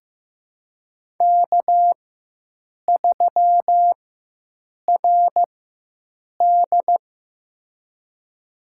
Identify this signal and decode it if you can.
Morse code, heard as a keyed tone, about 15 wpm, spaced as for 7 wpm, 700 Hz. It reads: K3RD
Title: USA Call Signs - 4x Character Spacing - 15wpm